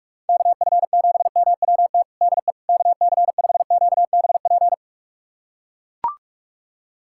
Morse code, heard as a keyed tone, about 42 wpm, 700 Hz. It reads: KF7MWT DE KC5QBP
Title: USA Callsign DE Callsign - 6 Characters - 45wpm